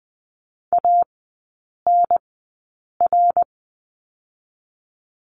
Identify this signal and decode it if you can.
Morse code, heard as a keyed tone, about 20 wpm, spaced as for 8 wpm, 700 Hz. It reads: ANR
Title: 053 Lesson 10e - Letter r - Three Characters 20wpm